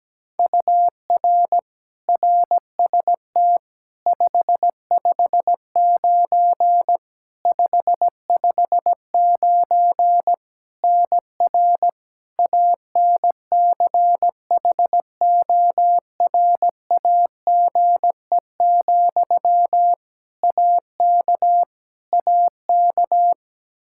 Morse code, heard as a keyed tone, about 17 wpm, 700 Hz. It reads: UR RST 559 559 NR ANCHORAGE, AK AK
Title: Practice QSO - Exchange 1 - Beginning - 17wpm